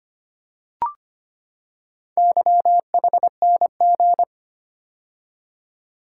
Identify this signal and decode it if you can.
Morse code, heard as a keyed tone, about 25 wpm, 700 Hz. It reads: YHNG
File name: practice / 25wpm / Four Letter-Number Combinations - 25wpm